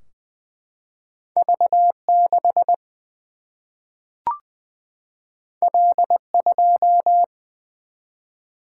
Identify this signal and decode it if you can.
Morse code, heard as a keyed tone, about 20 wpm, 700 Hz. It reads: V6 L2
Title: Single-Letter Single-Number 20wpm